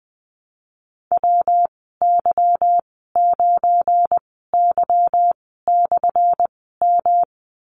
Morse code, heard as a keyed tone, about 20 wpm, spaced as for 15 wpm, 700 Hz. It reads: WY9Y/M
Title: 179 Lesson 29g - Slash - Call Signs 20wpm